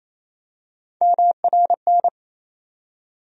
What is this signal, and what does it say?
Morse code, heard as a keyed tone, about 28 wpm, 700 Hz. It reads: MRN
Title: USA Call Sign Postfixes - 3 Letters 28wpm